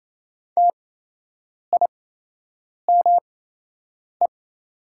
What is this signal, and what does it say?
Morse code, heard as a keyed tone, about 28 wpm, spaced as for 8 wpm, 700 Hz. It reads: TIME
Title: 8x Character Spacing - Top 500 Words - 28wpm